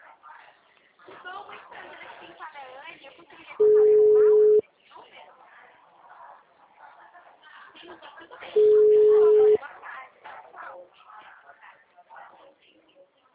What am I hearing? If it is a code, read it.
Ringback tone (Europe, 425 Hz), with speech.
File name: out- -2004- - - .32